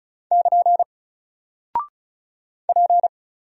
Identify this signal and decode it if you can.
Morse code, heard as a keyed tone, about 36 wpm, 700 Hz. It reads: <KN> P
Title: Single Character (Letter:Number:Punctuation:Prosign) 35wpm